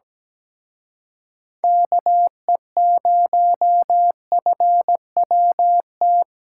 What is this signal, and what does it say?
Morse code, heard as a keyed tone, about 20 wpm, 700 Hz. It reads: KE0FWT
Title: Speed Racing - USA 2x3 Call Signs 17wpm